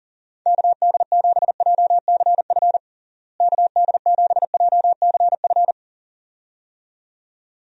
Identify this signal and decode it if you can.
Morse code, heard as a keyed tone, about 37 wpm, 700 Hz. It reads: KD7JCF KD7JCF
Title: USA Call Signs - 6 Characters - Repeated 2x - 40wpm